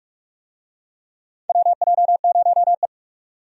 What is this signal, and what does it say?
Morse code, heard as a keyed tone, about 45 wpm, 700 Hz. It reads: WJ0E
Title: USA Call Signs - 4 Characters - 45wpm